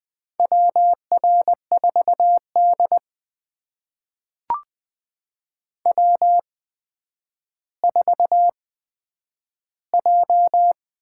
Morse code, heard as a keyed tone, about 21 wpm, spaced as for 10 wpm, 700 Hz. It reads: WR4D W4J
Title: USA Call Signs - 8x Character Spacing - 20wpm